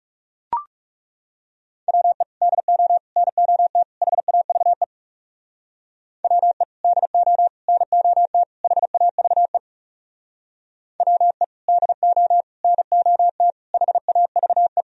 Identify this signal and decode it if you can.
Morse code, heard as a keyed tone, about 36 wpm, 700 Hz. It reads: WE DO NOT HAVE WE DO NOT HAVE WE DO NOT HAVE
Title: Speed-Racing - Sets of 4 Words 30wpm